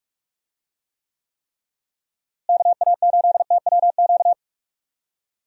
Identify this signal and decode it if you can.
Morse code, heard as a keyed tone, about 46 wpm, 700 Hz. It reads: KA8TWQ
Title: USA Call Signs - 6 Characters - Repeated 3x - 45wpm